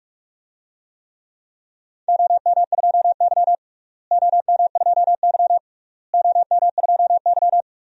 Morse code, heard as a keyed tone, about 46 wpm, 700 Hz. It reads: OM2Y OM2Y OM2Y
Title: DXCC-100 Call Signs - 4 Characters - Repeated 3x - 45wpm